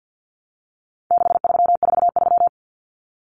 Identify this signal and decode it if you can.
Morse code, heard as a keyed tone, about 50 wpm, 700 Hz. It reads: 6343
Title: Numbers - 4-Digit 50wpm